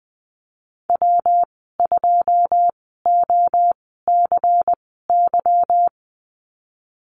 Morse code, heard as a keyed tone, about 20 wpm, spaced as for 14 wpm, 700 Hz. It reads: W2OCY
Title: 147 Lesson 24g - Letter y - Call Signs 20wpm